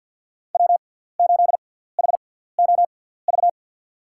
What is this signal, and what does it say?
Morse code, heard as a keyed tone, about 46 wpm, spaced as for 17 wpm, 700 Hz. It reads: W8HOV
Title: USA Call Signs - 6x Character Spacing - 50wpm